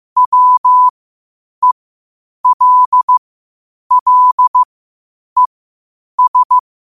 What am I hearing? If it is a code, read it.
Morse code, heard as a keyed tone, about 15 wpm, spaced as for 8 wpm, 1000 Hz. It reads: WELLES
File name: 8wpm